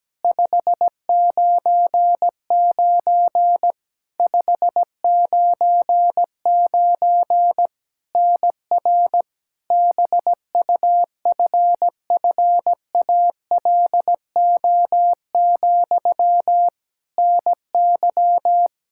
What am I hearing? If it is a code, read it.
Morse code, heard as a keyed tone, about 17 wpm, 700 Hz. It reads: 599 599 NR BUFFALO, NY
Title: K1USN SST QSO Practice 17wpm